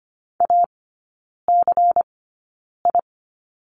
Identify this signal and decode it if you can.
Morse code, heard as a keyed tone, about 25 wpm, spaced as for 9 wpm, 700 Hz. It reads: ACI
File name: course / 25:10wpm / 093 Lesson 16e - Letter c - Three Characters 25wpm